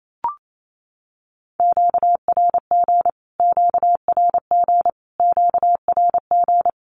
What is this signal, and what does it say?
Morse code, heard as a keyed tone, about 28 wpm, 700 Hz. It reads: QRG QRG QRG
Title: Q-Codes Repeated 3x 28wpm